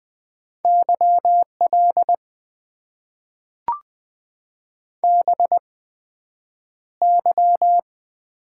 Morse code, heard as a keyed tone, about 20 wpm, spaced as for 9 wpm, 700 Hz. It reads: YL BY
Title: Top 90 QSO Elements - 8x Character Spacing - 20wpm